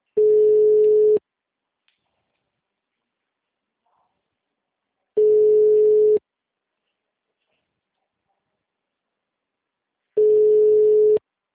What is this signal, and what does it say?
Ringback tone (Europe, 425 Hz).